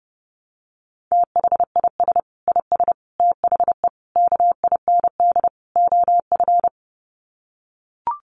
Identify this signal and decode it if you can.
Morse code, heard as a keyed tone, about 30 wpm, 700 Hz. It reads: THIS IS THE KIND OF
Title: Sets of 5 Words Encore 30wpm